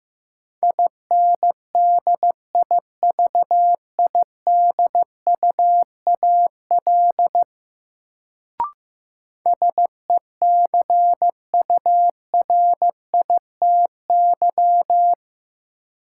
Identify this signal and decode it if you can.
Morse code, heard as a keyed tone, about 15 wpm, 700 Hz. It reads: INDIVIDUAL SECURITY
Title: Sentences from Top 1000 Words - Review Part 4 - New Word Drill - 15wpm